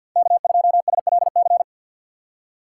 Morse code, heard as a keyed tone, about 49 wpm, 700 Hz. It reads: K2SLC
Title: USA Call Signs - 5 Characters - 6x Character Spacing - 50wpm